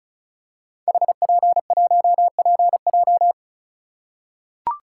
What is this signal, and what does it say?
Morse code, heard as a keyed tone, about 36 wpm, 700 Hz. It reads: HP1PJ
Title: Code Groups - 5 Letter:Numbers - 35wpm